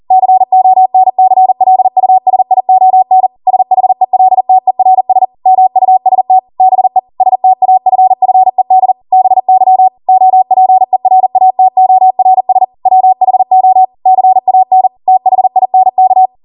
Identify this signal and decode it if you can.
Morse code, heard as a keyed tone, about 40 wpm, 750 Hz. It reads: CONCLUSION SHELTERS MUST BE STAFFED BY OPERATORS WHO CAN THINK